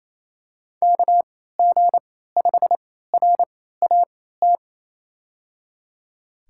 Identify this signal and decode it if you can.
Morse code, heard as a keyed tone, about 28 wpm, spaced as for 16 wpm, 700 Hz. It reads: KG5RAT
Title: USA Call Signs - 6 Characters - 3x Character Spacing - 28wpm